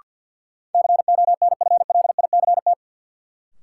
Morse code, heard as a keyed tone, about 52 wpm, 700 Hz. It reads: CONFLICT
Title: Sentences from Top 1300 Words - New Word Drill - 50wpm